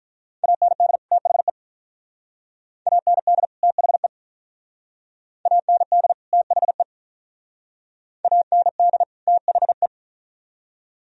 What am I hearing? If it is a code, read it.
Morse code, heard as a keyed tone, about 44 wpm, 690 Hz. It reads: AND THE AND THE AND THE AND THE